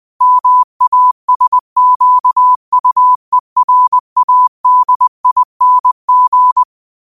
Morse code, heard as a keyed tone, about 20 wpm, 1000 Hz. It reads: MASQUERADING
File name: qotdfiles / 20wpm